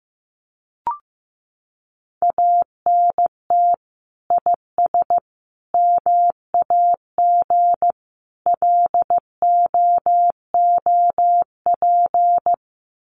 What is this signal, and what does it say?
Morse code, heard as a keyed tone, about 15 wpm, 700 Hz. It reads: ANT IS MAG LOOP
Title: Common QSO Phrases 15wpm